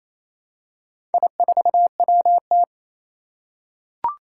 Morse code, heard as a keyed tone, about 27 wpm, 700 Hz. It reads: I4WT
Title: Four Challenging Letter-Number Combinations 28wpm